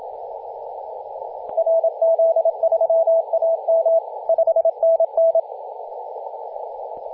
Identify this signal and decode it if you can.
Morse code, heard as a keyed tone, about 27 wpm, 660 Hz, with radio noise. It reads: RZ3AM 5NN
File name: e51wwa-10-104mhz-cw